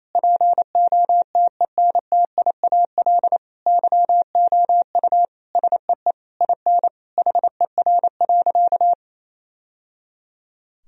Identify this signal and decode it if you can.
Morse code, heard as a keyed tone, about 28 wpm, 700 Hz. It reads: POTENTIAL YOU SEE IN HER.
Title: Sentences from Top 1000 Words - Review Part 4 - 28wpm